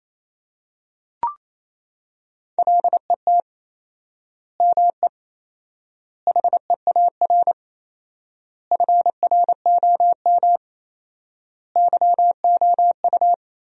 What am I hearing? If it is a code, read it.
Morse code, heard as a keyed tone, about 28 wpm, 700 Hz. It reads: LET ME HEAR FROM YOU
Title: 4x Word-Spacing - Sentences from Top 400 Words - 28wpm